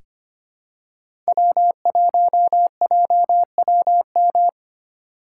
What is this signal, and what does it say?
Morse code, heard as a keyed tone, about 25 wpm, 700 Hz. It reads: W1JWM
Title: USA Call Signs - 5 Characters - 3x Character Spacing - 25wpm